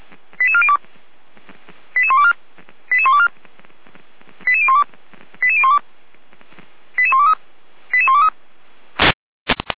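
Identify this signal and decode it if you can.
Five-tone selective call (ZVEI), heard as quick runs of five tones, tones of about 70 ms. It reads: 88402 88124 88124 88012 88012 88123 88123
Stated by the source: ZVEI five-tone selective call 88191 received by SDR.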